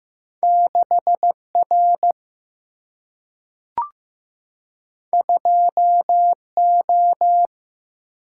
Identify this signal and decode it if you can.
Morse code, heard as a keyed tone, about 15 wpm, 700 Hz. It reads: 6 R 2 O
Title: Single-Letter Single-Number 15wpm